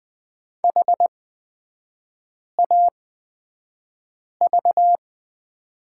Morse code, heard as a keyed tone, about 20 wpm, spaced as for 5 wpm, 700 Hz. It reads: HAV
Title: 236 Lesson 38e - Letter q - Three Characters 20wpm